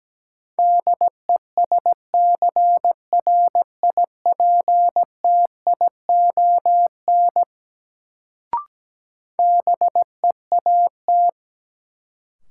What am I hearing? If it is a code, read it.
Morse code, heard as a keyed tone, about 17 wpm, 700 Hz. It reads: DESCRIPTION BEAT
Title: Sentences from Top 1300 Words - New Word Drill Alt - 17wpm